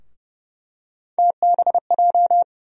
Morse code, heard as a keyed tone, about 32 wpm, 700 Hz. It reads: TBJ
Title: USA Call Sign Postfixes - 3 Letters 30wpm